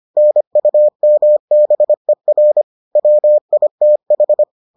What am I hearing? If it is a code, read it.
Morse code, heard as a keyed tone, about 25 wpm, 600 Hz. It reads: NUMBER WITH